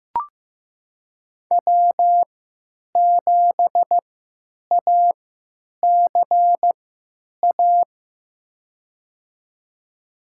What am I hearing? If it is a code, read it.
Morse code, heard as a keyed tone, about 15 wpm, spaced as for 9 wpm, 700 Hz. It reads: W7ACA
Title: USA Call Signs - 5 Characters - 3x Character Spacing - 15wpm